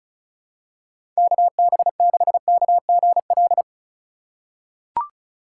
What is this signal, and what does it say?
Morse code, heard as a keyed tone, about 36 wpm, 700 Hz. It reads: KB6KGL
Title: USA Call Signs - 6 Characters - 4x Character Spacing - 35wpm